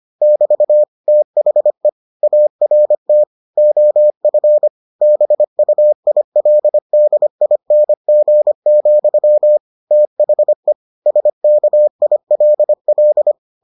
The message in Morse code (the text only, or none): <BT> THE ART OF BUILDING, THE SKILL